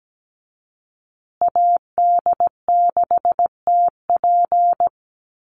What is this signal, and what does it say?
Morse code, heard as a keyed tone, about 17 wpm, 700 Hz. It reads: AD6TP
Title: USA Call Signs Repeated 3x 17wpm